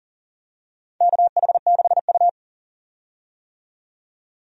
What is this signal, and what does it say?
Morse code, heard as a keyed tone, about 40 wpm, 700 Hz. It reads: KH6U
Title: DXCC-100 Call Signs - 4 Characters - 40wpm